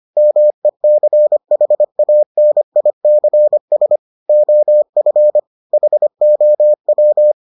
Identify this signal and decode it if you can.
Morse code, heard as a keyed tone, about 25 wpm, 600 Hz. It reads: MECHANICS OF HOW